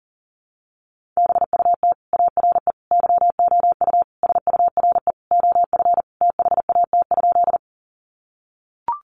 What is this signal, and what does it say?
Morse code, heard as a keyed tone, about 40 wpm, 700 Hz. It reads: BUT ARE YOU SURE OF THAT?
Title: Sentences from Top 300 Words 40wpm